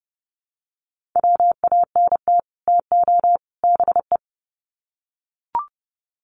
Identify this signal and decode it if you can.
Morse code, heard as a keyed tone, about 30 wpm, 700 Hz. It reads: WANT TO BE
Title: Sets of 3 Words 30wpm